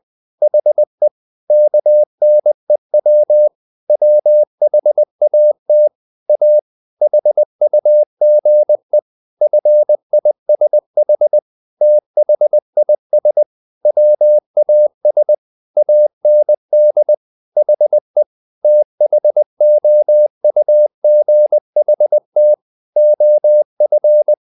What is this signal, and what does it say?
Morse code, heard as a keyed tone, about 20 wpm, 600 Hz. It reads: HE KNEW WHAT A HUGE FISH THIS WAS AND HE THOUGHT OF
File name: the old man and the sea / the old man and the sea part 1 20wpm